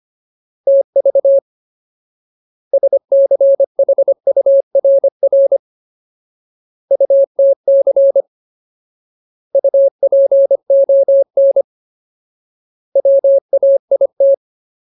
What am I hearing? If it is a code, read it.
Morse code, heard as a keyed tone, about 25 wpm, 550 Hz. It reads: TV SCHURR UTC UPON WAIT
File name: Ultimate Word List / 25WPM Ultimate